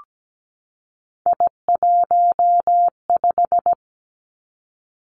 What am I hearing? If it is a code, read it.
Morse code, heard as a keyed tone, about 17 wpm, 700 Hz. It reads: I15
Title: Three Challenging Letter-Number Combinations 17wpm